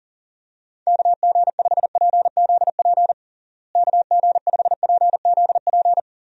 Morse code, heard as a keyed tone, about 37 wpm, 700 Hz. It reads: KG5PZP KG5PZP
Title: USA Call Signs - 6 Characters - Repeated 3x - 40wpm